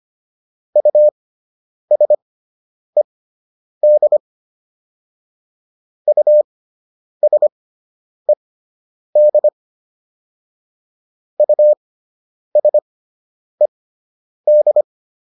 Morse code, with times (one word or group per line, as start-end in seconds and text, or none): 0.8-4.2 USED
6.1-9.5 USED
11.4-14.8 USED